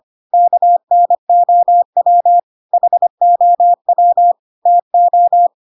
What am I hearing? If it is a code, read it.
Morse code, heard as a keyed tone, about 25 wpm, 700 Hz. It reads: KNOW HOW TO
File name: Pinocchio / Pinocchio ch3 25 25 ik1hge